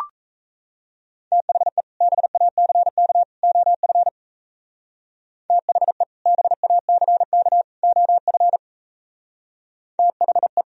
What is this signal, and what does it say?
Morse code, heard as a keyed tone, about 33 wpm, 700 Hz. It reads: THE BACK OF THE BACK OF THE
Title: Speed-Racing - Sets of 3 Words 28wpm